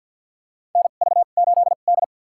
Morse code, heard as a keyed tone, about 52 wpm, spaced as for 36 wpm, 700 Hz. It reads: NV8D
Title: USA Call Signs - 4 Characters - 2x Character Spacing - 50wpm